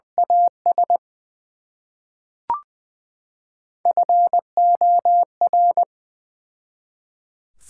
Morse code, heard as a keyed tone, about 21 wpm, 700 Hz. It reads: AS FOR
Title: Build A Phrase - 3-Word Phrases - 20wpm